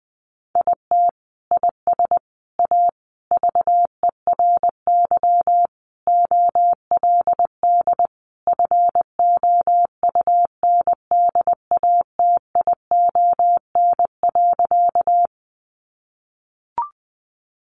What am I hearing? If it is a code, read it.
Morse code, heard as a keyed tone, about 20 wpm, 700 Hz. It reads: IT IS A VERY OLD FOUNDATION.
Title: Sentences from Top 1200 Words 20wpm